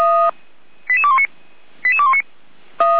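Five-tone selective call (ZVEI), heard as quick runs of five tones, tones of about 70 ms. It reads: 88219 88219
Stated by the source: ZVEI five-tone call received by SDR.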